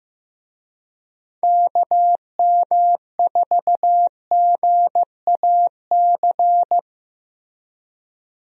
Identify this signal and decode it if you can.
Morse code, heard as a keyed tone, about 15 wpm, 700 Hz. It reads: KM4GAC
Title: USA Call Signs - 6 Characters - Repeated 3x - 15wpm